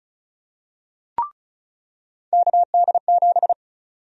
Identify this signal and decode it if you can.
Morse code, heard as a keyed tone, about 34 wpm, 700 Hz. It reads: KD7
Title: USA Call Sign Prefixes - 35wpm